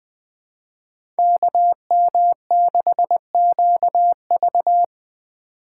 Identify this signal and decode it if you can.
Morse code, heard as a keyed tone, about 20 wpm, 700 Hz. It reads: KM6QV